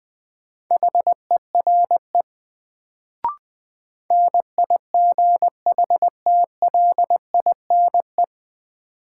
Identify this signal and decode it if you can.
Morse code, heard as a keyed tone, about 20 wpm, 700 Hz. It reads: HERE NIGHTLINE
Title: Sets of 5 Words - New Word Drill - 20wpm